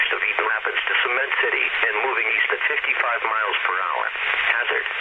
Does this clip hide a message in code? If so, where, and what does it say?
Yes: Emergency Alert System (SAME) tones (header bursts) from 0.4 to 4.7 s, heard as harsh buzzing bursts, with speech.